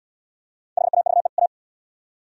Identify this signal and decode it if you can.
Morse code, heard as a keyed tone, about 74 wpm, 700 Hz. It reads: HAVE A